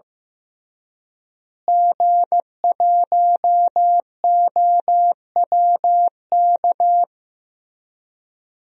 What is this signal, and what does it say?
Morse code, heard as a keyed tone, about 15 wpm, 700 Hz. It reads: G1OWK
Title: DXCC-100 Call Signs - 5 Characters - Repeated 3x - 15wpm